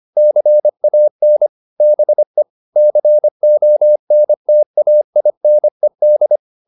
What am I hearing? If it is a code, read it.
Morse code, heard as a keyed tone, about 25 wpm, 600 Hz. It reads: CAN BE CONTAINED